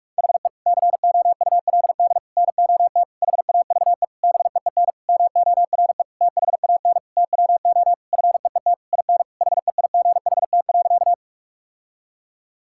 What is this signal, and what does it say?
Morse code, heard as a keyed tone, about 41 wpm, 700 Hz. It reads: HE COULD NOT HAVE BEEN MORE THAN TWO FEET IN HEIGHT.
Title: Sentences from Top 1600 Words 45wpm